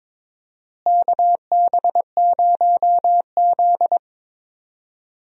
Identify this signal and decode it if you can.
Morse code, heard as a keyed tone, about 22 wpm, 700 Hz. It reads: KB0Z
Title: USA Call Signs - 4 Characters - Repeated 2x - 22wpm